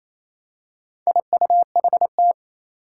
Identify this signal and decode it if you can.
Morse code, heard as a keyed tone, about 29 wpm, 700 Hz. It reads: IUHT
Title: Four Challenging Letter-Number Combinations 28wpm